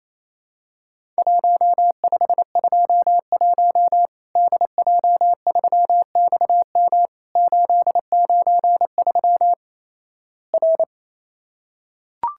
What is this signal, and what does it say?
Morse code, heard as a keyed tone, about 28 wpm, 700 Hz. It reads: 1521 DJ3XM 893 R
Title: Worked All Europe DX Contest - QTC Exchange - 28wpm